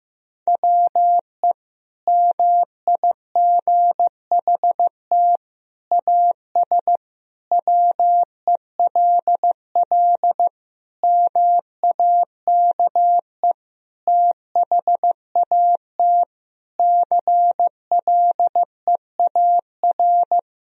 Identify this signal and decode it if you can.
Morse code, heard as a keyed tone, about 16 wpm, 700 Hz. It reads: WE MIGHT AS WELL MAKE THAT CLEAR
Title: Sentences from Top 1000 Words - Review Part 1 - 15wpm